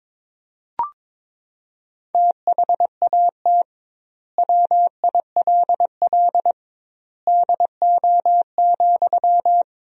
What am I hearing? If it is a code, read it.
Morse code, heard as a keyed tone, about 21 wpm, 700 Hz. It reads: THAT WILL DO,
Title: 2x Word-Spacing - Sentences from Top 100 Words - 22wpm